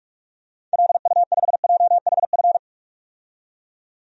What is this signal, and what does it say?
Morse code, heard as a keyed tone, about 44 wpm, 690 Hz. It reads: LU5JHF
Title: DXCC-100 Call Signs - 6 Characters - 45wpm